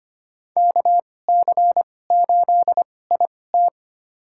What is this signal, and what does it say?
Morse code, heard as a keyed tone, about 25 wpm, spaced as for 18 wpm, 700 Hz. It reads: KC8IT
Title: USA Call Signs - 2x Character Spacing - 25wpm